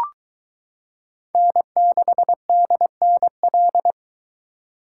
Morse code, heard as a keyed tone, about 23 wpm, 700 Hz. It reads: N6DNL